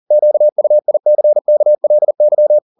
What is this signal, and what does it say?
Morse code, heard as a keyed tone, about 40 wpm, 600 Hz. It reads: QUICKLY